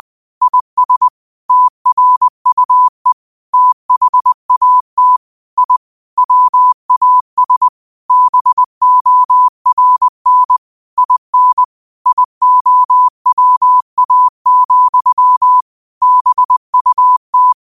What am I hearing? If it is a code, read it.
Morse code, heard as a keyed tone, about 20 wpm, 1000 Hz. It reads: IS TRUE THAT I WAS BORN IN IOWA, BUT
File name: qotdfiles / 20wpm